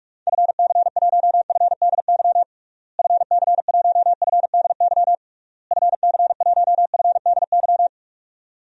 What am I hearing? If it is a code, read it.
Morse code, heard as a keyed tone, about 48 wpm, 700 Hz. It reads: FC1FDY FC1FDY FC1FDY
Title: DXCC-100 Call Signs - 6 Characters - Repeated 3x - 45wpm